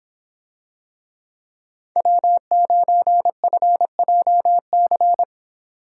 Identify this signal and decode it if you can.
Morse code, heard as a keyed tone, about 26 wpm, 700 Hz. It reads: W9FJC